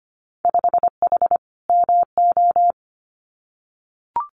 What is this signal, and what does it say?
Morse code, heard as a keyed tone, about 25 wpm, 700 Hz. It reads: 5H MO
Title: Winter Field Day Exchanges Encore 25wpm